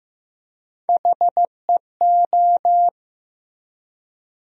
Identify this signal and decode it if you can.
Morse code, heard as a keyed tone, about 16 wpm, 700 Hz. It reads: HEO
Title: USA Call Sign Postfixes - 3 Letters 15wpm